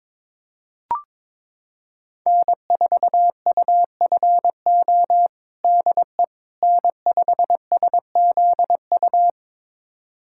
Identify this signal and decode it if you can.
Morse code, heard as a keyed tone, about 22 wpm, 700 Hz. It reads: N4UFO DE N5SZU
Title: USA Callsign DE Callsign - 5 Characters - 22wpm